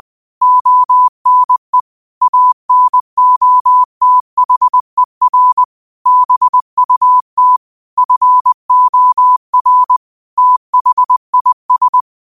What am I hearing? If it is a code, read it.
Morse code, heard as a keyed tone, about 20 wpm, 1000 Hz. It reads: ONE ANOTHER BUT FOR THIS